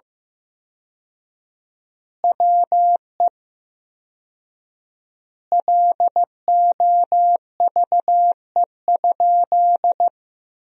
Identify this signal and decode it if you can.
Morse code, heard as a keyed tone, about 16 wpm, 700 Hz. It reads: WE LOVE?
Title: 4x Word-Spacing - Sentences from Top 400 Words - 15wpm